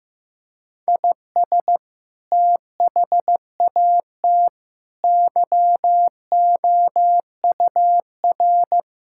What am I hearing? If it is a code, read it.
Morse code, heard as a keyed tone, about 16 wpm, 700 Hz. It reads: IS THAT YOUR
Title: Sentences from Top 1000 Words - Review Part 5 - 15wpm